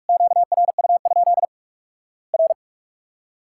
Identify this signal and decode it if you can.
Morse code, heard as a keyed tone, about 46 wpm, 700 Hz. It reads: QRU? R
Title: Worked All Europe DX Contest - QTC Exchange - 45wpm